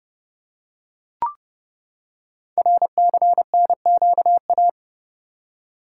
Morse code, heard as a keyed tone, about 30 wpm, 700 Hz. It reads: RCNQA
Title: Code Groups - 5 Letters - 30wpm